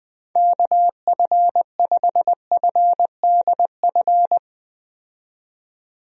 Morse code, heard as a keyed tone, about 20 wpm, 700 Hz. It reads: KF5FDF
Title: USA Call Signs - 6 Characters - 20wpm